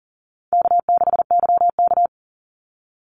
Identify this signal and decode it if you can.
Morse code, heard as a keyed tone, about 40 wpm, 700 Hz. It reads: K6YK
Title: NA SOTA Chaser Callsigns 40wpm